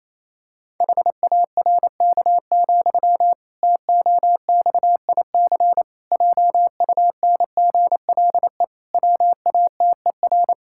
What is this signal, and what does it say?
Morse code, heard as a keyed tone, about 28 wpm, 700 Hz. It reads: HARK, TOXIC JUNGLE WATER
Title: Pangram Challenge v2 - 28wpm